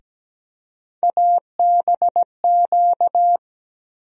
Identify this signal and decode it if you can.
Morse code, heard as a keyed tone, about 17 wpm, 700 Hz. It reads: ABQ
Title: USA Call Sign Postfixes - 3 Letters 17wpm